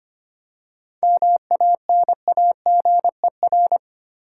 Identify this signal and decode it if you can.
Morse code, heard as a keyed tone, about 25 wpm, 700 Hz. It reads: MANAGER
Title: Seven Letter Words - 25wpm